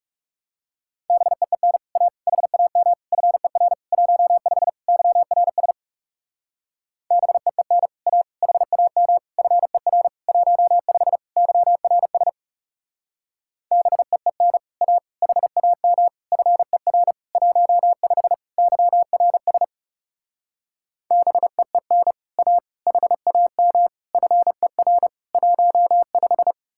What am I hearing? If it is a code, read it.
Morse code, heard as a keyed tone, about 36 wpm, 700 Hz. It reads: BEEN A HAM FER 15 YRS BEEN A HAM FER 15 YRS BEEN A HAM FER 15 YRS BEEN A HAM FER 15
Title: Speed-Racing Common QSO Phrases - 30wpm